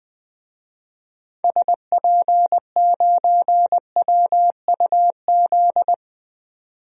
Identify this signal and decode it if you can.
Morse code, heard as a keyed tone, about 20 wpm, 700 Hz. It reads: SP9WUZ
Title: DXCC-100 Call Signs - 6 Characters - 20wpm